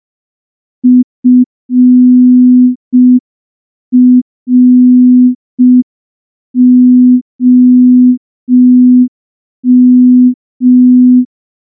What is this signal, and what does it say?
Morse code, heard as a keyed tone, about 5 wpm, 250 Hz. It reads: FROM
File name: morse / morse-training-05WPM